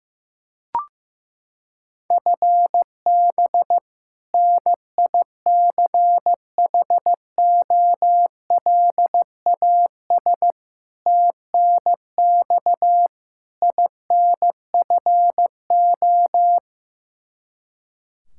Morse code, heard as a keyed tone, about 15 wpm, 700 Hz. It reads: FB NICHOLAS TNX INFO
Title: K1USN SST QSO Practice 15wpm